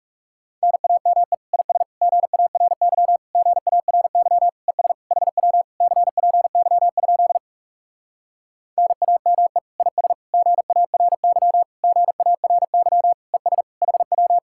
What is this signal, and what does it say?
Morse code, heard as a keyed tone, about 40 wpm, 700 Hz. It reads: NAME IS GARY GARY ES HW CPY? NAME IS GARY GARY ES HW